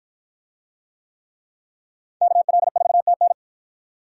Morse code, heard as a keyed tone, about 48 wpm, 700 Hz. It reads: KL4TN